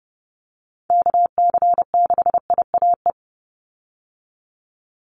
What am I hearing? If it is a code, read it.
Morse code, heard as a keyed tone, about 30 wpm, 700 Hz. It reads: KC6IAE
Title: USA Call Signs 30wpm